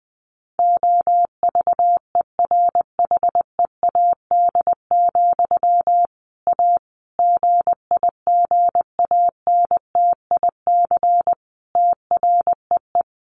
Morse code, heard as a keyed tone, about 20 wpm, 700 Hz. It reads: OVERHEAD, A GIGANTIC TREE
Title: A Sound Of Thunder 20wpm